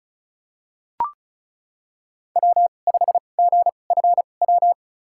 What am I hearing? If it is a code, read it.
Morse code, heard as a keyed tone, about 36 wpm, spaced as for 25 wpm, 700 Hz. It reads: W5GFW
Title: USA Call Signs - 2x Character Spacing - 35wpm